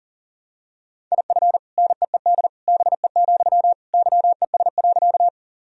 Morse code, heard as a keyed tone, about 40 wpm, 700 Hz. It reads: IF NEED BE, YES.
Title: Sentences from Top 1000 Words - Review Part 1 - 40wpm